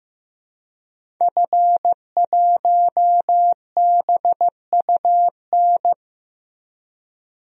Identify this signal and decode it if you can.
Morse code, heard as a keyed tone, about 15 wpm, 700 Hz. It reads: F1BUN
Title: DXCC-100 Call Signs - 5 Characters - Repeated 3x - 15wpm